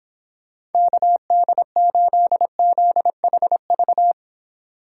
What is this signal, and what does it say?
Morse code, heard as a keyed tone, about 29 wpm, 700 Hz. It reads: KD8ZHV